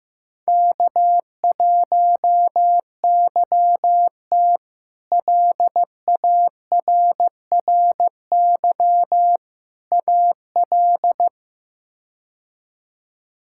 Morse code, heard as a keyed tone, about 15 wpm, 700 Hz. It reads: K1YT LARRY AL
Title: K1USN SST QSO Practice - 15wpm